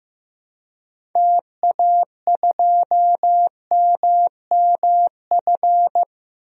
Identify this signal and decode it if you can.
Morse code, heard as a keyed tone, about 16 wpm, 700 Hz. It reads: TA2MMF
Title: DXCC-100 Call Signs - 6 Characters - Repeated 3x - 15wpm